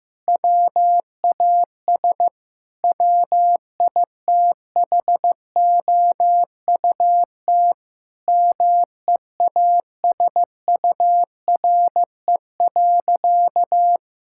Morse code, heard as a keyed tone, about 15 wpm, 700 Hz. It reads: WAS WITHOUT MEASURE.
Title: Sentences from Top 1000 Words - Review Part 6 - 15wpm